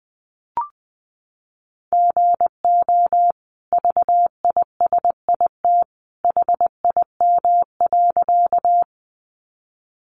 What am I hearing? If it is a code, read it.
Morse code, heard as a keyed tone, about 20 wpm, 700 Hz. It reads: GO VISIT HIM.
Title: Sentences from Top 900 Words 20wpm